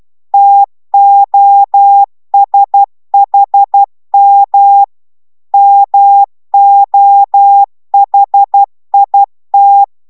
Morse code, heard as a keyed tone, about 12 wpm, 800 Hz. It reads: TOSHM MOHIT